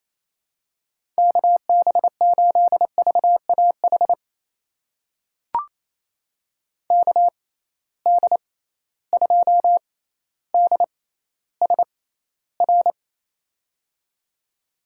Morse code, heard as a keyed tone, about 27 wpm, spaced as for 14 wpm, 700 Hz. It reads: KB8VAH KD2DSR
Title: USA Call Signs - 6 Characters - 6x Character Spacing - 28wpm